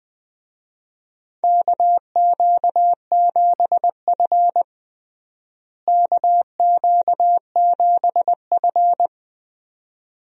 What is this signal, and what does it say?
Morse code, heard as a keyed tone, about 20 wpm, 700 Hz. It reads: KQ7F KQ7F
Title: USA Call Signs - 4 Characters - Repeated 2x - 20wpm